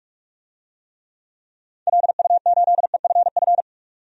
Morse code, heard as a keyed tone, about 46 wpm, 700 Hz. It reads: LU8EFF
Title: DXCC-100 Call Signs - 6 Characters - 45wpm